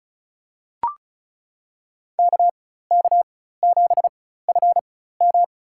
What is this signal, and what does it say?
Morse code, heard as a keyed tone, about 36 wpm, spaced as for 16 wpm, 700 Hz. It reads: KK7FM
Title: USA Call Signs - 4x Character Spacing - 35wpm